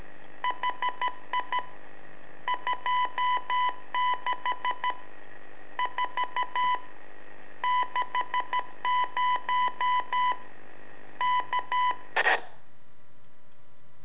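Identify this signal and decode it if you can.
Morse code, heard as a keyed tone, about 13 wpm, 1000 Hz. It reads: HI 26 4 60 K
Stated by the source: Telemetry in Morse from LU7AA and LU4EZ, AMSAT and Turma Mar del Plata, sent from a balloon.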